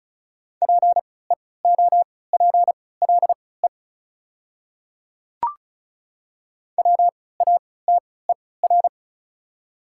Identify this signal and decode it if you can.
Morse code, heard as a keyed tone, about 36 wpm, spaced as for 20 wpm, 690 Hz. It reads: PEOPLE WATER
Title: 3x Character Spacing - Top 100 Words - 35wpm